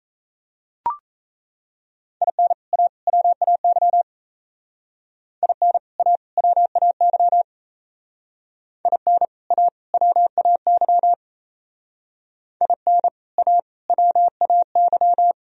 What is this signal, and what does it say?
Morse code, heard as a keyed tone, about 36 wpm, 700 Hz. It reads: IN A WAY IN A WAY IN A WAY IN A WAY